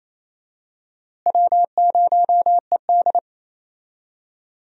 Morse code, heard as a keyed tone, about 28 wpm, 700 Hz. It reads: W0ED
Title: USA Call Signs - 4 Characters - Repeated 3x - 28wpm